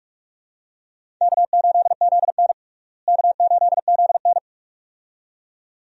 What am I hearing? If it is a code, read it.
Morse code, heard as a keyed tone, about 42 wpm, 700 Hz. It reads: K8ZN K8ZN
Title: USA Call Signs - 4 Characters - Repeated 2x - 45wpm